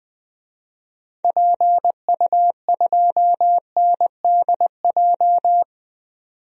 Morse code, heard as a keyed tone, about 20 wpm, 700 Hz. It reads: PU2NDJ
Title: DXCC-100 Call Signs - 6 Characters - 20wpm